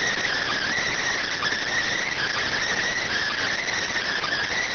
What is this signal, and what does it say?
Emergency Alert System (SAME) tones, heard as harsh buzzing bursts. It is heard from 0.5 to 3.7 s.